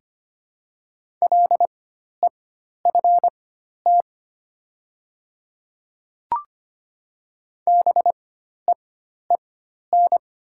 Morse code, heard as a keyed tone, about 25 wpm, spaced as for 12 wpm, 700 Hz. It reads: LEFT BEEN
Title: 4x Character Spacing - Top 300 Words - 25wpm